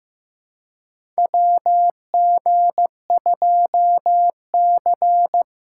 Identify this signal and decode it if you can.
Morse code, heard as a keyed tone, about 15 wpm, 700 Hz. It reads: WG2C
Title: USA Call Signs - 2x Character Spacing - 15wpm